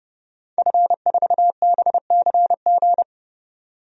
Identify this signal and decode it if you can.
Morse code, heard as a keyed tone, about 32 wpm, 700 Hz. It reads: F4BCG
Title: DXCC-100 Call Signs - 5 Characters - 30wpm